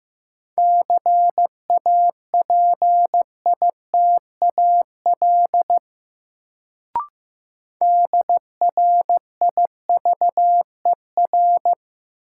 Morse code, heard as a keyed tone, about 15 wpm, 700 Hz. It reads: CAPITAL DRIVER
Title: Sentences from Top 1000 Words - New Word Drill - 15wpm